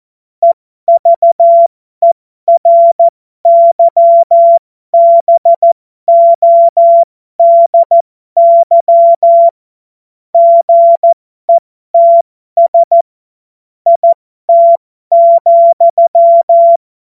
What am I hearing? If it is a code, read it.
Morse code, heard as a keyed tone, about 14 wpm, spaced as for 12 wpm, 680 Hz. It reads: EVERYBODY GETS IT,
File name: Morsekurs / Witze